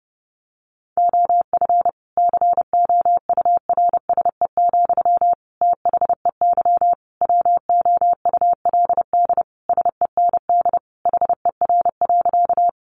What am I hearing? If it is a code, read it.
Morse code, heard as a keyed tone, about 30 wpm, 700 Hz. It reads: OF COURSE, THEY WOULD SEND HER.
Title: Sentences from Top 1500 Words 30wpm